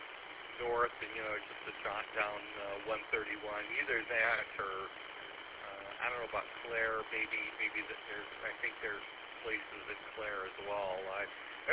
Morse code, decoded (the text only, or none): none